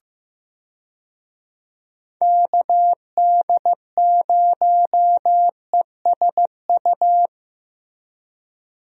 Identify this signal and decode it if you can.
Morse code, heard as a keyed tone, about 15 wpm, 700 Hz. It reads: KD0ESU
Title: USA Call Signs - 6 Characters - Repeated 3x - 15wpm